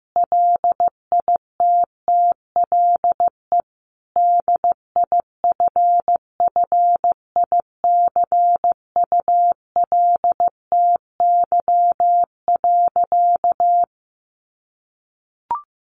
Morse code, heard as a keyed tone, about 15 wpm, 700 Hz. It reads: LITTLE DIFFICULTY.
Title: Sentences from Top 1500 Words 15wpm